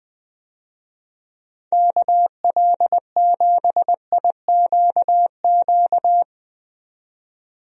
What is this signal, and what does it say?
Morse code, heard as a keyed tone, about 20 wpm, 700 Hz. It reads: KL7IQQ
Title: DXCC-100 Call Signs - 6 Characters - 20wpm